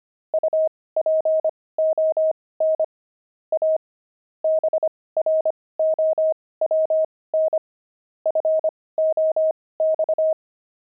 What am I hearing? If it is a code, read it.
Morse code, heard as a keyed tone, about 25 wpm, spaced as for 18 wpm, 630 Hz. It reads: UPON A BROWN FOX
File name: pangram-fs-18-cs-25-jay